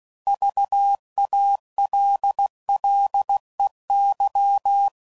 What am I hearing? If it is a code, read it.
Morse code, heard as a keyed tone, about 16 wpm, 800 Hz. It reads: VALLEY
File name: CW Code Groups @15WPM